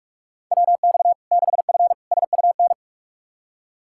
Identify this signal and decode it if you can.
Morse code, heard as a keyed tone, about 43 wpm, 700 Hz. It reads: WX 6F SUN
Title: Practice QSO - Exchange 2 - Info - 45wpm